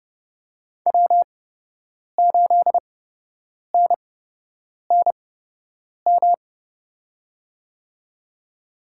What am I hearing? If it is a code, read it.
Morse code, heard as a keyed tone, about 30 wpm, spaced as for 8 wpm, 700 Hz. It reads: W8NNM
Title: USA Call Signs - 5 Characters - 8x Character Spacing - 30wpm